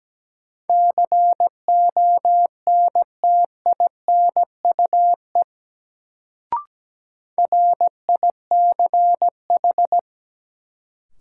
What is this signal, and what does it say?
Morse code, heard as a keyed tone, about 17 wpm, 700 Hz. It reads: CONTINUE RICH
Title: Sentences from Top 1100 Words - New Word Drill Alt - 17wpm